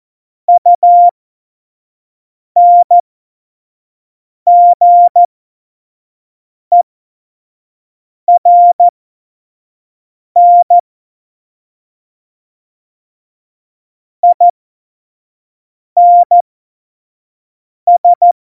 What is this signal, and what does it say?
Morse code, heard as a keyed tone, about 14 wpm, spaced as for 5 wpm, 700 Hz. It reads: UNGERN INS
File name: Morsekurs / Witze / Witz 12. Januar 2021, 25 F70